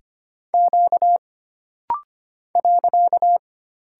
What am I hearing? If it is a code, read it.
Morse code, heard as a keyed tone, about 25 wpm, 700 Hz. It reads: Q .
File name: Single Character (Letter:Number:Punctuation:Prosign) - Rapid-Fire - 25wpm